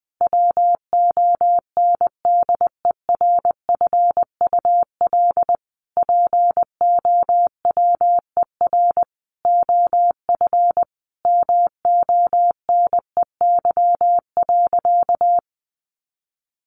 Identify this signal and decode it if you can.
Morse code, heard as a keyed tone, about 20 wpm, 700 Hz. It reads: WONDERFUL POWER OF MONEY.
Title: Sentences from Top 1100 Words 20wpm